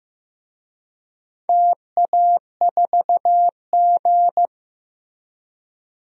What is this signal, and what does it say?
Morse code, heard as a keyed tone, about 15 wpm, 700 Hz. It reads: TA4G
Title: DXCC-100 Call Signs - 4 Characters - Repeated 2x - 15wpm